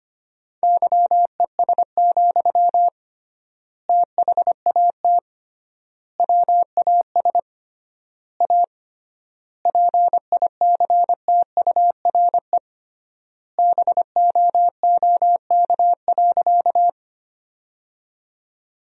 Morse code, heard as a keyed tone, about 25 wpm, 700 Hz. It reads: YES, THAT WAS A PICTURE BOOK.
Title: 3x Word-Spacing - Sentences from Top 500 Words - 25wpm